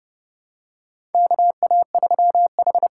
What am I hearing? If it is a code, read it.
Morse code, heard as a keyed tone, about 30 wpm, 700 Hz. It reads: KA3H
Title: USA Call Signs - 6 Characters - 8x Character Spacing - 30wpm